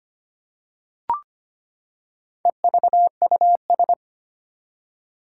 Morse code, heard as a keyed tone, about 25 wpm, 700 Hz. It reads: EVUS
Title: Four Challenging Letter-Number Combinations 25wpm